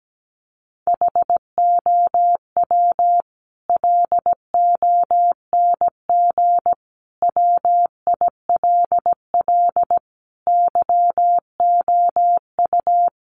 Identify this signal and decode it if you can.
Morse code, heard as a keyed tone, about 17 wpm, 700 Hz. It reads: HOW LONG WILL YOU
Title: Sentences from Top 1200 Words 17wpm